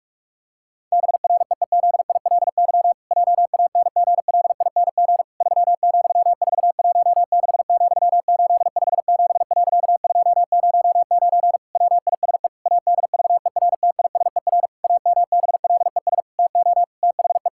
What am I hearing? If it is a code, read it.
Morse code, heard as a keyed tone, about 41 wpm, 700 Hz. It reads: BREEZILY JANGLING 3,416,857.209 WISE ADVERTISER AMBLES TO THE
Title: Pangram Challenge v2 - 45wpm